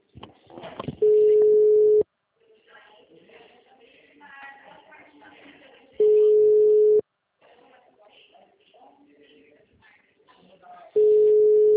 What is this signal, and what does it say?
Ringback tone (Europe, 425 Hz), with speech.